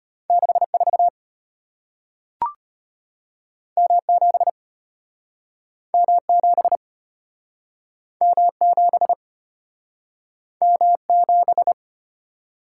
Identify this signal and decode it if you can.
Morse code, heard as a keyed tone, about 35 wpm, 700 Hz. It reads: 64 M7 M7 M7 M7